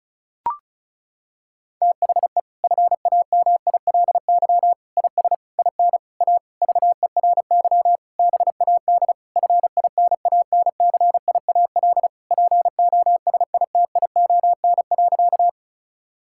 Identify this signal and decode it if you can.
Morse code, heard as a keyed tone, about 36 wpm, 700 Hz. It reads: THE FAMILY IS IN A VERY BAD FINANCIAL POSITION.
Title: Sentences from Top 1000 Words - Review Part 5 - 35wpm